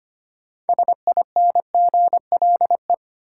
Morse code, heard as a keyed tone, about 25 wpm, 700 Hz. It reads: SINGLE
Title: Sentences from Top 700 Words - New Word Drill - 25wpm